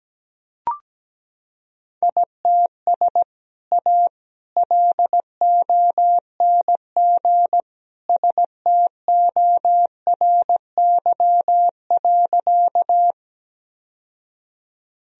Morse code, heard as a keyed tone, about 17 wpm, 700 Hz. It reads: ITS A LONG STORY.
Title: Sentences from Top 1000 Words - Review Part 1 - 17wpm